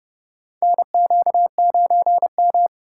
Morse code, heard as a keyed tone, about 30 wpm, 700 Hz. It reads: NQ9M
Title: USA Call Signs - 2x Character Spacing - 30wpm